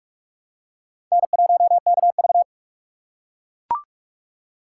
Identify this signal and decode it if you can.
Morse code, heard as a keyed tone, about 39 wpm, 700 Hz. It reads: N1KV